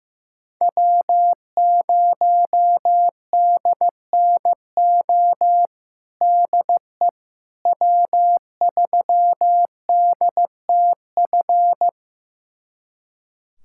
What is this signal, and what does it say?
Morse code, heard as a keyed tone, about 16 wpm, 700 Hz. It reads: W0DNO DE W3DTF
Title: USA Callsign DE Callsign - 5 Characters - 15wpm